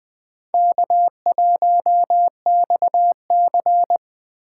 Morse code, heard as a keyed tone, about 19 wpm, 700 Hz. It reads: K1XC